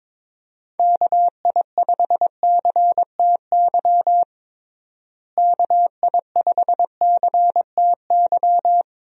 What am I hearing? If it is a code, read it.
Morse code, heard as a keyed tone, about 22 wpm, 700 Hz. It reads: KI5CTY KI5CTY
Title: USA Call Signs - 6 Characters - Repeated 3x - 22wpm